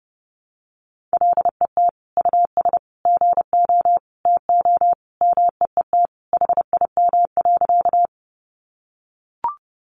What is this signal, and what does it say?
Morse code, heard as a keyed tone, about 30 wpm, 700 Hz. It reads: LET US GO TO MEET HIM.
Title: Sentences from Top 1100 Words 30wpm